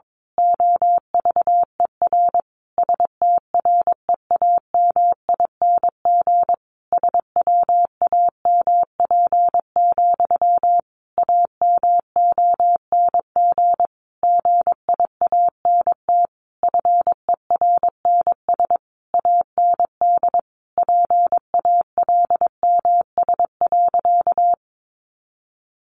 Morse code, heard as a keyed tone, about 22 wpm, 700 Hz. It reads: OVER STREAMING SWAMP, AMONG GIANT FERNS AND PALMS.
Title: A Sound Of Thunder 22wpm